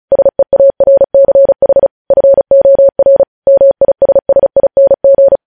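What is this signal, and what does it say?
Morse code, heard as a keyed tone, about 36 wpm, 550 Hz. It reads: SEARCH FOR MISSING
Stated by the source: Morse code practice reading news headlines at 35 words per minute.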